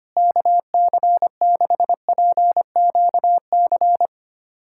Morse code, heard as a keyed tone, about 25 wpm, 700 Hz. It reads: KC6PQC
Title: USA Callsign DE Callsign 25wpm